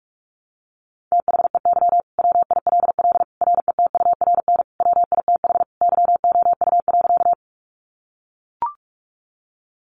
Morse code, heard as a keyed tone, about 44 wpm, 700 Hz. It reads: THEY WILL RETURN WITH YOU.
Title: Sentences from Top 800 Words 45wpm